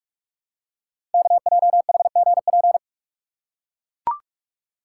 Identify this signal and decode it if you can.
Morse code, heard as a keyed tone, about 44 wpm, 700 Hz. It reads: KJHGP
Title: Code Groups - 5 Letters - 45wpm